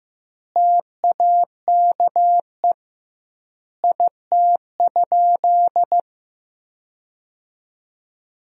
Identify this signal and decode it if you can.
Morse code, heard as a keyed tone, about 16 wpm, 700 Hz. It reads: TAKE IT?
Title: 2x Word-Spacing - Sentences from Top 300 Words - 15wpm